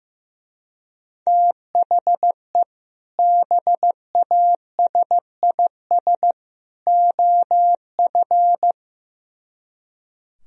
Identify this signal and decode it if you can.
Morse code, heard as a keyed tone, about 15 wpm, 700 Hz. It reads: THE BASIS OF
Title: Speed-Racing - Sets of 3 Words 15wpm